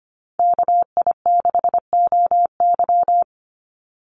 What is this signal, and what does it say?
Morse code, heard as a keyed tone, about 25 wpm, 700 Hz. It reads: KI6OY
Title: Call Signs Repeated 2x - 25wpm